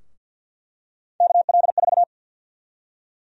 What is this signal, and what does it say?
Morse code, heard as a keyed tone, about 53 wpm, 700 Hz. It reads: KL4
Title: USA Call Sign Prefixes - Repeated 2x - 50wpm